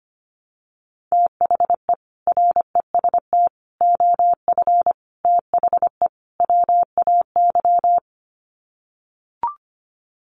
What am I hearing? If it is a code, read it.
Morse code, heard as a keyed tone, about 25 wpm, 700 Hz. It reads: THE REST OF THE WAY
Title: Sets of 5 Words Encore 25wpm